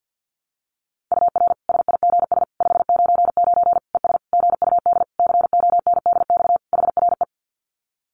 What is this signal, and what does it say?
Morse code, heard as a keyed tone, about 50 wpm, 700 Hz. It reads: UR SIGS 599 ES GUD CONDX HRE
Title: Common QSO Phrases 50wpm